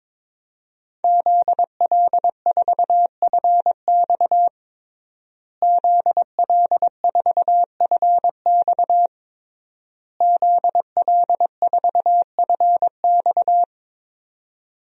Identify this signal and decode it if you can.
Morse code, heard as a keyed tone, about 22 wpm, 700 Hz. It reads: ZL4FX ZL4FX ZL4FX
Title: DXCC-100 Call Signs - 5 Characters - Repeated 3x - 22wpm